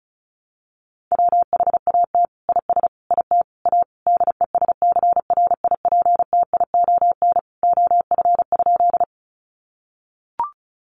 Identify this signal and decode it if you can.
Morse code, heard as a keyed tone, about 34 wpm, 700 Hz. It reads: WHAT IS IT A DESCRIPTION OF?
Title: Sentences from Top 1300 Words 35wpm